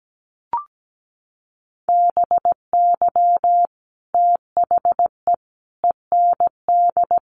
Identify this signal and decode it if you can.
Morse code, heard as a keyed tone, about 17 wpm, 700 Hz. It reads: BY THE END
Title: Sets of 5 Words Encore 17wpm